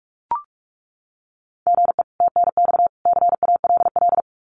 Morse code, heard as a keyed tone, about 44 wpm, 700 Hz. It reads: GE TNX CALL
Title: Common QSO Phrases 45wpm